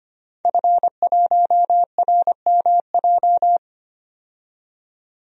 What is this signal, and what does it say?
Morse code, heard as a keyed tone, about 25 wpm, 700 Hz. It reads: F1RMJ
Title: DXCC-100 Call Signs - 5 Characters - 25wpm